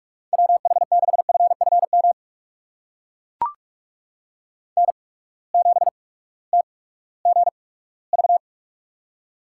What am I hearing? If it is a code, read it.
Morse code, heard as a keyed tone, about 44 wpm, spaced as for 20 wpm, 700 Hz. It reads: WH6FFM N7TGV
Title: USA Call Signs - 8x Character Spacing - 45wpm